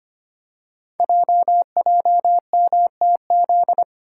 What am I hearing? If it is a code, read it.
Morse code, heard as a keyed tone, about 25 wpm, 700 Hz. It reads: JJMTZ
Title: Code Groups - 5 Letters - 25wpm